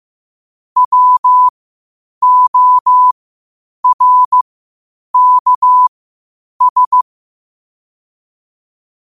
Morse code, heard as a keyed tone, about 15 wpm, spaced as for 8 wpm, 1000 Hz. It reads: WORKS
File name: qotdfiles / 8wpm